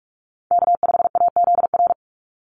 Morse code, heard as a keyed tone, about 44 wpm, 700 Hz. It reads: K5AZR
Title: Call Signs Repeated 2x - 45wpm